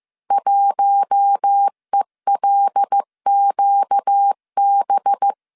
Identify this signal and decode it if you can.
Morse code, heard as a keyed tone, about 15 wpm, 800 Hz. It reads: 1ELQB
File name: CW / DET / 1 6 1500C S C 14wpm 2951